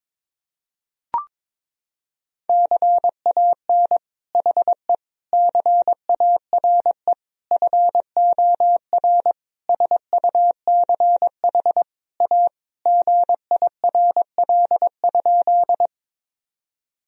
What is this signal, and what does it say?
Morse code, heard as a keyed tone, about 22 wpm, 700 Hz. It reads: CAN HE CARE FOR SUCH A GIRL?
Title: Sentences from Top 1000 Words - Review Part 1 - 22wpm